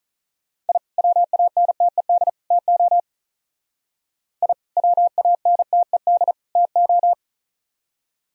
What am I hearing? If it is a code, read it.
Morse code, heard as a keyed tone, about 42 wpm, 700 Hz. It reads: I WANTED TO I WANTED TO
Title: Speed-Racing - Sets of 3 Words 35wpm